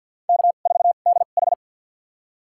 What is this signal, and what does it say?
Morse code, heard as a keyed tone, about 52 wpm, spaced as for 37 wpm, 700 Hz. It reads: K4DH
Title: USA Call Signs - 4 Characters - 2x Character Spacing - 50wpm